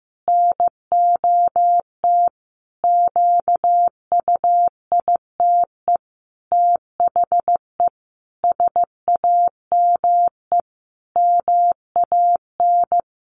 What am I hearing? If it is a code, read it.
Morse code, heard as a keyed tone, about 15 wpm, 700 Hz. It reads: NOT QUITE THE SAME MAN
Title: A Sound Of Thunder 15wpm